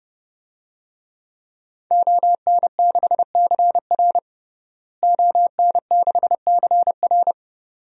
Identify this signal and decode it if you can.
Morse code, heard as a keyed tone, about 30 wpm, 700 Hz. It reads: ON6CR ON6CR
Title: DXCC-100 Call Signs - 5 Characters - Repeated 2x - 30wpm